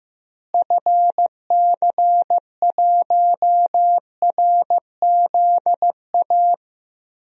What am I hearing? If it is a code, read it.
Morse code, heard as a keyed tone, about 15 wpm, 700 Hz. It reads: FC1RZA
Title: DXCC-100 Call Signs - 6 Characters - Repeated 2x - 15wpm